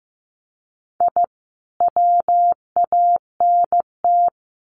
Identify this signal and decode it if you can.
Morse code, heard as a keyed tone, about 15 wpm, 700 Hz. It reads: I WANT
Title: Sets of 5 Words Encore 15wpm